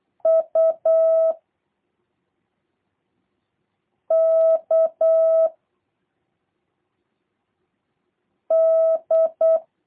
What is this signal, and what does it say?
Morse code, heard as a keyed tone, about 8 wpm, 650 Hz. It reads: U K D
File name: jh8cbh / 2023cw / review1-3